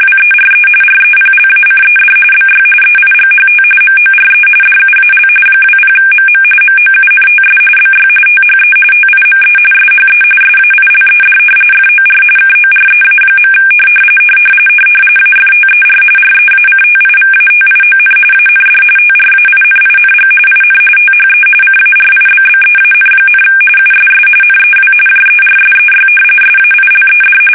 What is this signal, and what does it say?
RTTY, about 45.45 baud. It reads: WERE REPLACED BY EIGHT MORE DUAL 20MM MOUNTS BY THE TIME SHE SAILED FOR KOREA. AFTER REACTIVATION AND TRAINING IN THE CARIBBEAN, USS NEW JERSEY SAILED
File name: fsk850 / battleshipnj